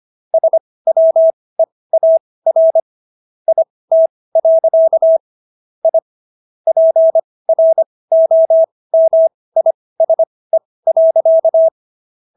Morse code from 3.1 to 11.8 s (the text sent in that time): IT. I PROMISE.